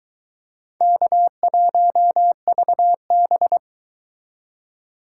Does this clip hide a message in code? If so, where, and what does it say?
Yes: Morse code from 0.8 to 3.6 s, heard as a keyed tone, about 22 wpm, 700 Hz. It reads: K1VB